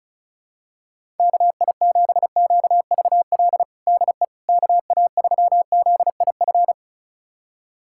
Morse code, heard as a keyed tone, about 36 wpm, 700 Hz. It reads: KI7QVL DE KA3ZIF
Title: USA Callsign DE Callsign 35wpm